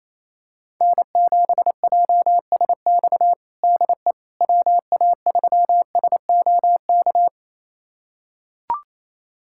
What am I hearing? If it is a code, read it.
Morse code, heard as a keyed tone, about 28 wpm, 700 Hz. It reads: N7JSX DE WA3SOK
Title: USA Callsign DE Callsign 28wpm